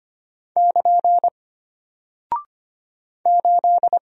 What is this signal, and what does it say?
Morse code, heard as a keyed tone, about 25 wpm, 700 Hz. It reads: <KN> 8
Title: Single Character (Letter:Number:Punctuation:Prosign) Alt 25wpm